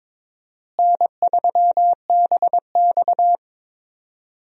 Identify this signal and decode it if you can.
Morse code, heard as a keyed tone, about 22 wpm, 700 Hz. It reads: N3BX
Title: USA Call Signs - 4 Characters - Repeated 2x - 22wpm